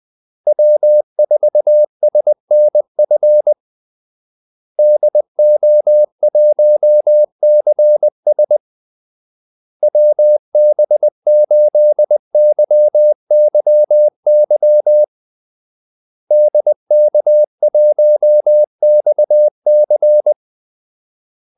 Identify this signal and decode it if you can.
Morse code, heard as a keyed tone, about 20 wpm, 600 Hz. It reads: W4SNF DO1CS WB8YYY DK1XC
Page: Kaldesignaler 20 wpm